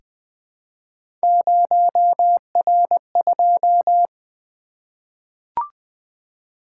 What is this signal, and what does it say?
Morse code, heard as a keyed tone, about 20 wpm, 700 Hz. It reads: 0R2
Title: Three Letter-Number Combinations - 20wpm